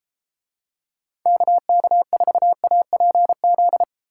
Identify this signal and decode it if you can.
Morse code, heard as a keyed tone, about 32 wpm, 700 Hz. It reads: KK4APZ